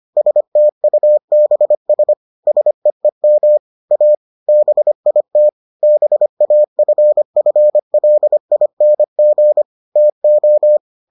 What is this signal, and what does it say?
Morse code, heard as a keyed tone, about 25 wpm, 600 Hz. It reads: STUBS SEEM A BIT BAFFLING TO